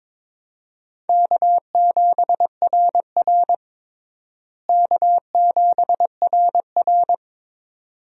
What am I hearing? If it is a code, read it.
Morse code, heard as a keyed tone, about 22 wpm, 700 Hz. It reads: K7RR K7RR
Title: USA Call Signs - 4 Characters - Repeated 2x - 22wpm